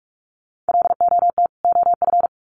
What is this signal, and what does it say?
Morse code, heard as a keyed tone, about 44 wpm, 700 Hz. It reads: LOT OF
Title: Sets of 2 Words 45wpm